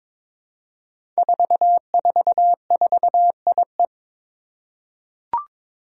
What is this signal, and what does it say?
Morse code, heard as a keyed tone, about 22 wpm, 700 Hz. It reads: 444IE
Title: Five Challenging Letter-Number Combinations 22wpm